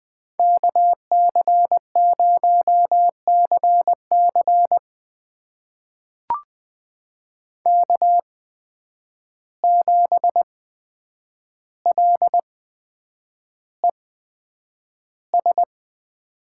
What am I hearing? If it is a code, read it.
Morse code, heard as a keyed tone, about 20 wpm, spaced as for 9 wpm, 700 Hz. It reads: KC0CC K7LES
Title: USA Call Signs - 5 Characters - 8x Character Spacing - 20wpm